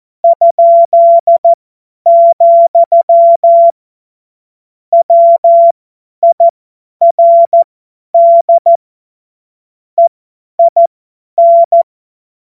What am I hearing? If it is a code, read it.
Morse code, heard as a keyed tone, about 14 wpm, spaced as for 10 wpm, 680 Hz. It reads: ?, WIRD EIN
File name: Witz-12-August-2025-50-F70